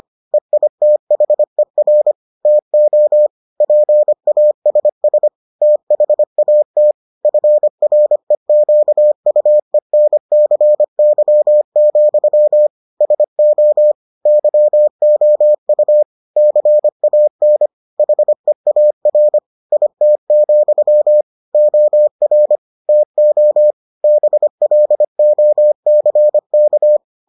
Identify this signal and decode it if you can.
Morse code, heard as a keyed tone, about 25 wpm, 600 Hz. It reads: EITHER TO PASS THAT FREQUENCY, SO YOU CAN HEAR IT, OR TO BLOCK